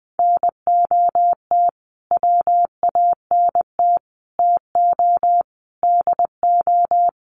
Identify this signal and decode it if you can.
Morse code, heard as a keyed tone, about 20 wpm, 700 Hz. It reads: NOT WANT TO DO
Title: Sets of 5 Words Encore 20wpm